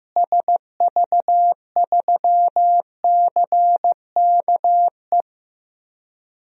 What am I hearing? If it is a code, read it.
Morse code, heard as a keyed tone, about 15 wpm, 700 Hz. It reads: SV3CKE
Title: DXCC-100 Call Signs - 6 Characters - Repeated 3x - 15wpm